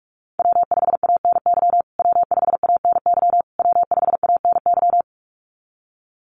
Morse code, heard as a keyed tone, about 44 wpm, 700 Hz. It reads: W5ANY W5ANY W5ANY
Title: USA Call Signs Repeated 3x 45wpm